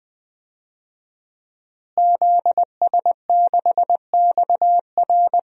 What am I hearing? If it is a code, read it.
Morse code, heard as a keyed tone, about 20 wpm, 700 Hz. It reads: ZS6XR
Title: DXCC-100 Call Signs - 5 Characters - Repeated 3x - 20wpm